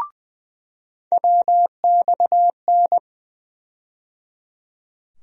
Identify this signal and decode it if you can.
Morse code, heard as a keyed tone, about 20 wpm, 700 Hz. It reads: WXN
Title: USA Call Sign Postfixes - 3 Letters 20wpm